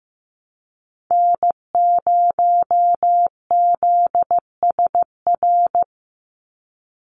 Morse code, heard as a keyed tone, about 15 wpm, 700 Hz. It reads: N0ZSR
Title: USA Call Signs 15wpm